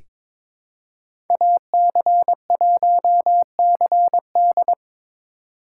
Morse code, heard as a keyed tone, about 21 wpm, 700 Hz. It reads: AC1CD
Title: USA Call Signs - 2x Character Spacing - 22wpm